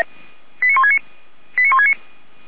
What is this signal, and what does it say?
Five-tone selective call (ZVEI), heard as quick runs of five tones, tones of about 70 ms. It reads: 89169 89169
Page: SDR/ZVEI